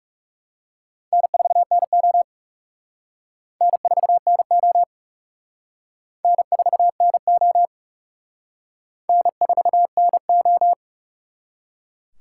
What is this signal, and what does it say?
Morse code, heard as a keyed tone, about 36 wpm, 700 Hz. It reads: N4NO N4NO N4NO N4NO
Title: Speed Racing - USA 1x2 Call Signs 30wpm